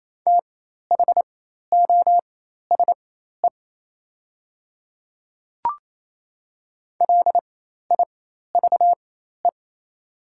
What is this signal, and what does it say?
Morse code, heard as a keyed tone, about 28 wpm, spaced as for 13 wpm, 700 Hz. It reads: THOSE LIVE
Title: 4x Character Spacing - Top 300 Words - 28wpm